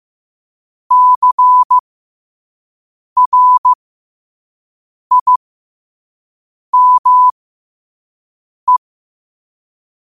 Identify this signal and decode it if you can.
Morse code, heard as a keyed tone, about 15 wpm, spaced as for 5 wpm, 1000 Hz. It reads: CRIME
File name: qotdfiles / 5wpm